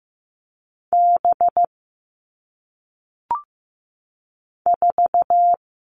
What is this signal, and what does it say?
Morse code, heard as a keyed tone, about 15 wpm, 700 Hz. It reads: B 4
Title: All Cut Numbers - 1-Digit 15wpm